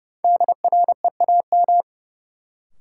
Morse code, heard as a keyed tone, about 30 wpm, 700 Hz. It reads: DREAM
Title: Sentences from Top 1500 Words - New Word Drill - 30wpm